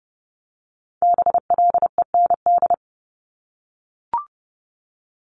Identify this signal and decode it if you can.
Morse code, heard as a keyed tone, about 30 wpm, 700 Hz. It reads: BLEND
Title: Multi-letter Phonemes 30wpm